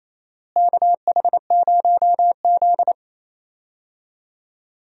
Morse code, heard as a keyed tone, about 28 wpm, 700 Hz. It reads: KH0Z
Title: USA Call Signs - 4 Characters - Repeated 3x - 28wpm